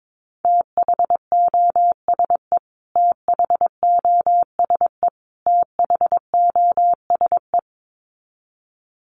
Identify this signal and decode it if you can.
Morse code, heard as a keyed tone, about 22 wpm, 700 Hz. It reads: THOSE THOSE THOSE
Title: Top 500 Words Repeated 22wpm